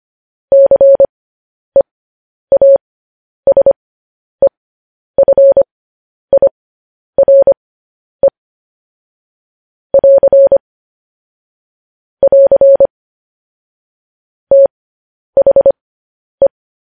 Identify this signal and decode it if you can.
Morse code, heard as a keyed tone, about 25 wpm, spaced as for 10 wpm, 550 Hz. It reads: CEASEFIRE <AR> <AR> THE
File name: podcast / News headlines-1x-25x10